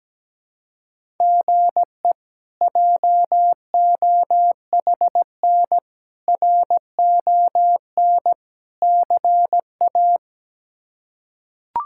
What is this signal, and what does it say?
Morse code, heard as a keyed tone, about 17 wpm, 700 Hz. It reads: GE JOHN RON CA
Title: K1USN SST QSO Practice - 17wpm